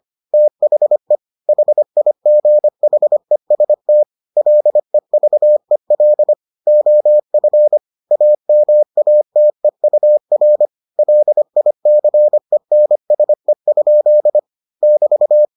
Morse code, heard as a keyed tone, about 25 wpm, 600 Hz. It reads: THE HIGHEST LEVEL OF AMATEUR LICENSE? <BT>